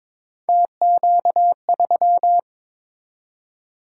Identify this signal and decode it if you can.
Morse code, heard as a keyed tone, about 22 wpm, 700 Hz. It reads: TQ3
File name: practice / 22wpm / Three Letter-Number Combinations - 22wpm